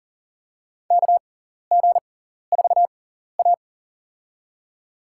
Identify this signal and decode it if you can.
Morse code, heard as a keyed tone, about 40 wpm, spaced as for 14 wpm, 700 Hz. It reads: KG4A
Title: USA Call Signs - 4 Characters - 6x Character Spacing - 40wpm